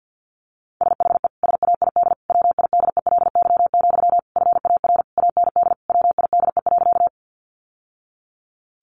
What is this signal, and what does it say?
Morse code, heard as a keyed tone, about 49 wpm, 700 Hz. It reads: SHE SAID WIDELY, FAR AND WIDE.
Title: Sentences from Top 1200 Words 50wpm